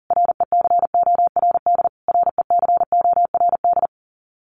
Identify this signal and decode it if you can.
Morse code, heard as a keyed tone, about 40 wpm, 700 Hz. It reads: RECORD RECORD
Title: Top 500 Words Repeated 40wpm